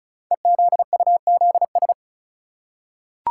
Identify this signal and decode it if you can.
Morse code, heard as a keyed tone, about 36 wpm, 700 Hz. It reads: EZUZS
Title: Code Groups - 5 Letters - 35wpm